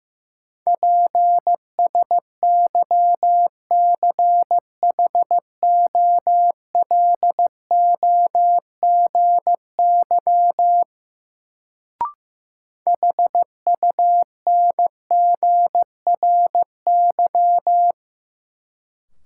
Morse code, heard as a keyed tone, about 15 wpm, 700 Hz. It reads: PSYCHOLOGY HUNGRY
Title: Sentences from Top 1400 Words - New Word Drill - 15wpm